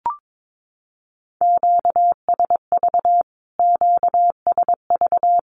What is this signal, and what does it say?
Morse code, heard as a keyed tone, about 22 wpm, 700 Hz. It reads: QSV QSV
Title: Q-Codes Repeated 3x 22wpm